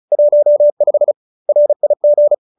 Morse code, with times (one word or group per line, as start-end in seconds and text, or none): 0.1-1.1 15
1.5-2.3 RIG